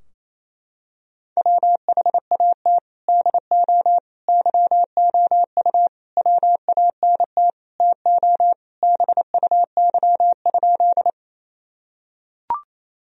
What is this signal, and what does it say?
Morse code, heard as a keyed tone, about 28 wpm, 700 Hz. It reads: WHAT DO YOU WANT TO BUY?
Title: Sentences from Top 1000 Words - Review Part 4 - 28wpm